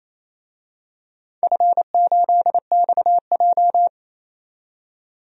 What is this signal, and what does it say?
Morse code, heard as a keyed tone, about 28 wpm, 700 Hz. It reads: F8XJ
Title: DXCC-100 Call Signs - 4 Characters - 28wpm